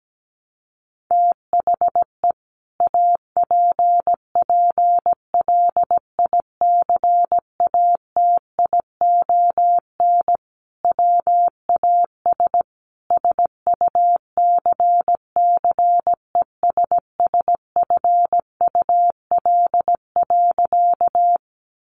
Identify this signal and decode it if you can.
Morse code, heard as a keyed tone, about 17 wpm, 700 Hz. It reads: THE APPLICATION WAS SUCCESSFUL.
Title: Sentences from Top 1200 Words 17wpm